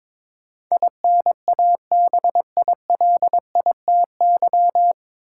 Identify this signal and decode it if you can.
Morse code, heard as a keyed tone, about 22 wpm, 700 Hz. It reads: INABILITY
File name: 9 Letter Words - First 3 Letters Given - 22wpm